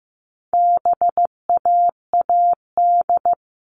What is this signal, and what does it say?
Morse code, heard as a keyed tone, about 15 wpm, 700 Hz. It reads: BAAD
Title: All Cut Numbers - 4-Digit 15wpm